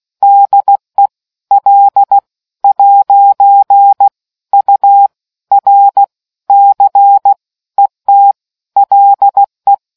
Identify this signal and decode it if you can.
Morse code, heard as a keyed tone, about 16 wpm, spaced as for 13 wpm, 800 Hz. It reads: DE L'URC ET LE